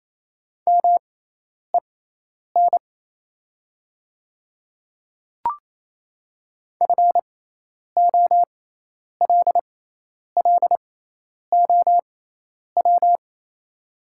Morse code, heard as a keyed tone, about 28 wpm, spaced as for 10 wpm, 700 Hz. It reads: MEN FOLLOW
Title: 6x Character Spacing - Top 300 Words - 28wpm